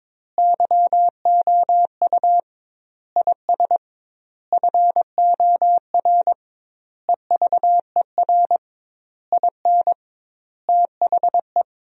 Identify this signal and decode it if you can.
Morse code, heard as a keyed tone, about 22 wpm, 700 Hz. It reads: YOU IS FOR EVER IN THE
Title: 2x Word-Spacing - Sentences from Top 400 Words - 22wpm